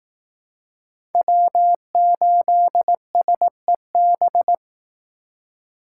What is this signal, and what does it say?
Morse code, heard as a keyed tone, about 18 wpm, 700 Hz. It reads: W8SEB
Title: Speed Racing - USA 1x3 Call Signs 15wpm